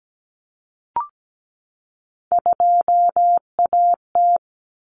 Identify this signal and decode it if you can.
Morse code, heard as a keyed tone, about 17 wpm, 700 Hz. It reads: 2AT
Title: Common Cut Numbers - 3-Digit 17wpm